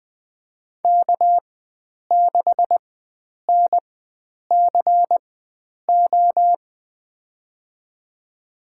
Morse code, heard as a keyed tone, about 20 wpm, spaced as for 9 wpm, 700 Hz. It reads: K6NCO
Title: USA Call Signs - 5 Characters - 4x Character Spacing - 20wpm